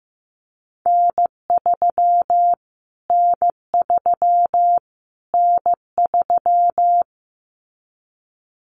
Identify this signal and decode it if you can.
Morse code, heard as a keyed tone, about 15 wpm, 700 Hz. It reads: N3 N3 N3
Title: Callsign Prefixes for Top 25 DXCC - Repeated 3x - 15wpm